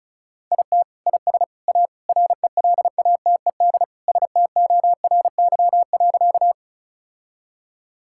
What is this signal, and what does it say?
Morse code, heard as a keyed tone, about 34 wpm, 700 Hz. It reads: IT IS A RELATED STORY.
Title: Sentences from Top 1000 Words - Review Part 5 - 35wpm